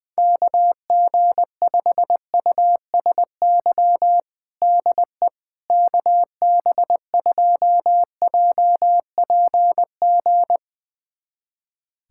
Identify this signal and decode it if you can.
Morse code, heard as a keyed tone, about 20 wpm, 700 Hz. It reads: KG5USY DE KB2JPG
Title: USA Callsign DE Callsign - 6 Characters - 20wpm